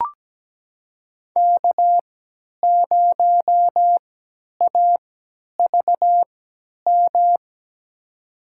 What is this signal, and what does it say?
Morse code, heard as a keyed tone, about 17 wpm, spaced as for 10 wpm, 700 Hz. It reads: K0AVM
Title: USA Call Signs - 5 Characters - 3x Character Spacing - 17wpm